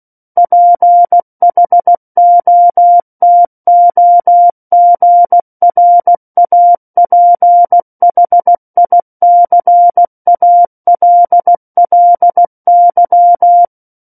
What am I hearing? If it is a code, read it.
Morse code, heard as a keyed tone, about 16 wpm, 700 Hz. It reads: PHOTOGRAPHICALLY
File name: Book One / 07-Chapter 7 - 16 WPM